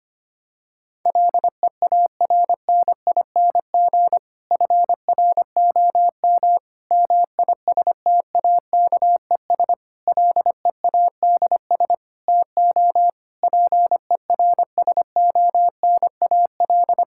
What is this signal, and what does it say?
Morse code, heard as a keyed tone, about 25 wpm, 700 Hz. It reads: LEARNING FROM MISTAKES LEADS TO PERSONAL
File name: practice / 25wpm / Difficult Sentences with Hints - 25wpm